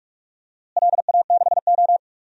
Morse code, heard as a keyed tone, about 46 wpm, 700 Hz. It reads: LA6O
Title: DXCC-100 Call Signs - 4 Characters - Repeated 2x - 45wpm